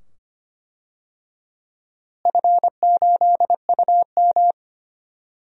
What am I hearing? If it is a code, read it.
Morse code, heard as a keyed tone, about 25 wpm, 700 Hz. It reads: F8UM
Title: DXCC-100 Call Signs - 4 Characters - Repeated 2x - 25wpm